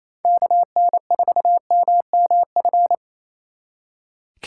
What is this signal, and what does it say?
Morse code, heard as a keyed tone, about 33 wpm, 700 Hz. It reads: KN4MMF
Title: Speed Racing - USA 2x3 Call Signs 28wpm